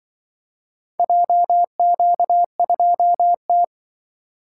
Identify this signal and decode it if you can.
Morse code, heard as a keyed tone, about 25 wpm, 700 Hz. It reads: JQ2T